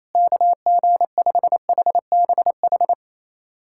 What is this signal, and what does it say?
Morse code, heard as a keyed tone, about 28 wpm, 700 Hz. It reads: KG5HBH
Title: USA Callsign DE Callsign - 6 Characters - 28wpm